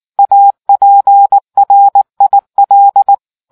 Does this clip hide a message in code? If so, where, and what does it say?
Yes: Morse code from 0.2 to 3.1 s, heard as a keyed tone, about 19 wpm, 800 Hz. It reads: APRIL